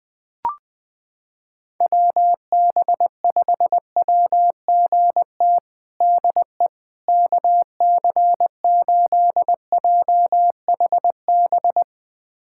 Morse code, heard as a keyed tone, about 20 wpm, 700 Hz. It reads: WB5WGT DE KC8JHB
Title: USA Callsign DE Callsign 20wpm